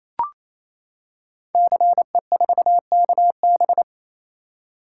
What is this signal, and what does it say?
Morse code, heard as a keyed tone, about 27 wpm, 700 Hz. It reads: CE4KB
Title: Code Groups - 5 Letter:Numbers - 28wpm